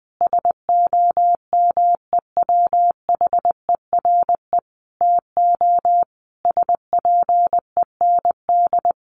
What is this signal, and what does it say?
Morse code, heard as a keyed tone, about 20 wpm, 700 Hz. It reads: SOMEWHERE TO SPEND
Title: Sentences from Top 1400 Words 20wpm